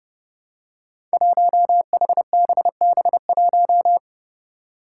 Morse code, heard as a keyed tone, about 30 wpm, 700 Hz. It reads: 1HBB1
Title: Five Challenging Letter-Number Combinations 30wpm